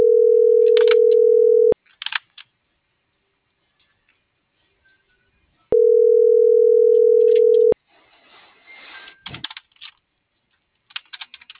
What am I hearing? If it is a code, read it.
Ringback tone (North America), 440 + 480 Hz.
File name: CENTRAL MABE / out sebastian.espinosa